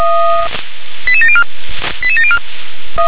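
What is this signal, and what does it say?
Five-tone selective call (ZVEI), heard as quick runs of five tones, tones of about 70 ms. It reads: 88704 88704
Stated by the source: ZVEI five-tone call received by SDR.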